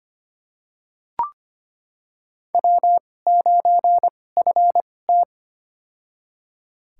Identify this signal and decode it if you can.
Morse code, heard as a keyed tone, about 25 wpm, spaced as for 18 wpm, 700 Hz. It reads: W9FT
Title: USA Call Signs - 4 Characters - 2x Character Spacing - 25wpm